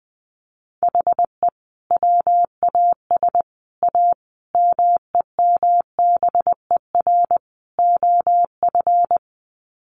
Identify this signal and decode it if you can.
Morse code, heard as a keyed tone, about 19 wpm, 700 Hz. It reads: HE WAS A MEMBER OF
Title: Sets of 5 Words Encore2 20wpm